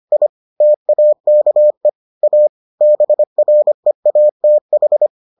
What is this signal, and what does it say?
Morse code, heard as a keyed tone, about 25 wpm, 600 Hz. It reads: I TAKE A BREATH